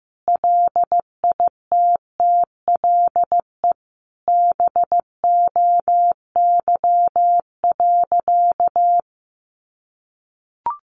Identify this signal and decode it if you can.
Morse code, heard as a keyed tone, about 15 wpm, 700 Hz. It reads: LITTLE BOY.
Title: Sentences from Top 1000 Words 15wpm